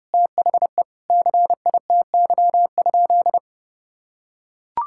Morse code, heard as a keyed tone, about 30 wpm, 700 Hz. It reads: THE CITY?
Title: Sentences from Top 1000 Words - Review Part 5 - 30wpm